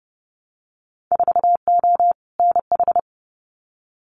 Morse code, heard as a keyed tone, about 32 wpm, 700 Hz. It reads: 4O NH
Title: Winter Field Day Exchanges Encore 30wpm